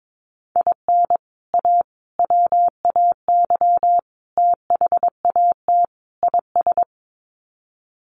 Morse code, heard as a keyed tone, about 22 wpm, 700 Hz. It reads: IN A WAY THAT IS
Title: Sets of 5 Words Encore 22wpm